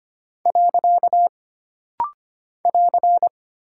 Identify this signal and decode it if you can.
Morse code, heard as a keyed tone, about 25 wpm, 700 Hz. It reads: . <AR>
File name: Single Punctuation-Prosign - Rapid-Fire - 25wpm